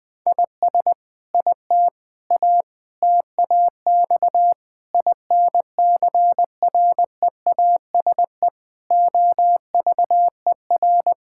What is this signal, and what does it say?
Morse code, heard as a keyed tone, about 20 wpm, 700 Hz. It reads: IS IT A TAX INCREASE OVER
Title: Sentences from Top 1000 Words - Review Part 5 - 20wpm